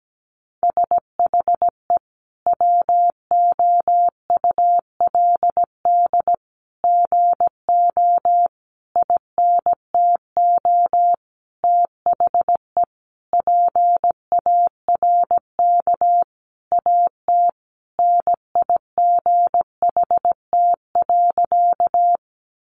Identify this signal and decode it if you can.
Morse code, heard as a keyed tone, about 17 wpm, 700 Hz. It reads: SHE WOULD GO INTO THE PARK AT NIGHT.
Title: Sentences from Top 1300 Words 17wpm